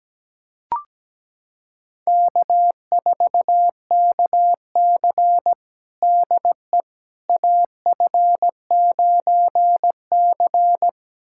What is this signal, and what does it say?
Morse code, heard as a keyed tone, about 17 wpm, 700 Hz. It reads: K4KC DE AF9C
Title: USA Callsign DE Callsign - 4 Characters - 17wpm